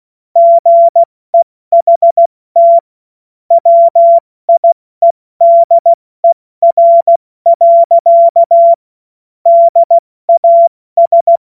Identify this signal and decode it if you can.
Morse code, heard as a keyed tone, about 16 wpm, 680 Hz. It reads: GEHT WIEDER. DAS
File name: Morsekurs / Witze / Witz 12 März 2024 70 F80